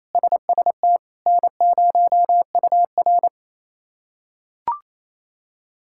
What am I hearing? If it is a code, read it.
Morse code, heard as a keyed tone, about 29 wpm, 700 Hz. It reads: SST N0UR
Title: K1USN SST QSO Practice - 28wpm